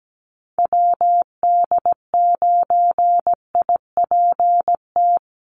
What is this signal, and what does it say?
Morse code, heard as a keyed tone, about 17 wpm, 700 Hz. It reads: WD9IPT
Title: USA Call Signs Repeated 3x 17wpm